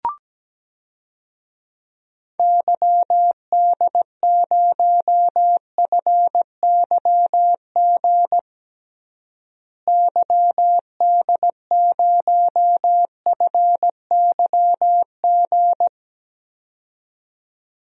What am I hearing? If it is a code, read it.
Morse code, heard as a keyed tone, about 17 wpm, 700 Hz. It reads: YD0FYG YD0FYG
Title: DXCC-100 Call Signs - 6 Characters - Repeated 2x - 17wpm